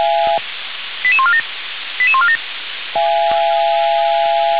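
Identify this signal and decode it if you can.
Five-tone selective call (ZVEI), heard as quick runs of five tones, tones of about 70 ms. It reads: 88147 88147
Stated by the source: ZVEI five-tone selective call received via SDR, from Feuerwehr Orlamünde (fire brigade) audio server.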